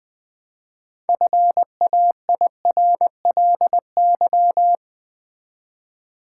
Morse code, heard as a keyed tone, about 20 wpm, 700 Hz. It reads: FAIRLY
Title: Sentences from Top 1100 Words - New Word Drill Alt v2 - 20wpm